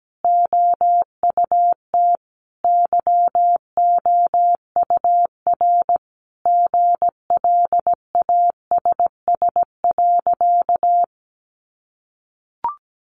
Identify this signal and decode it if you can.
Morse code, heard as a keyed tone, about 17 wpm, 700 Hz. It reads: OUT YOUR GLASS.
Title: Sentences from Top 1300 Words 17wpm